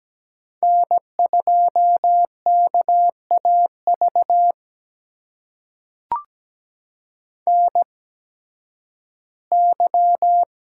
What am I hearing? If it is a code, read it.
Morse code, heard as a keyed tone, about 17 wpm, spaced as for 11 wpm, 700 Hz. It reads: N2KAV NY
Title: USA Call Signs - 8x Character Spacing - 17wpm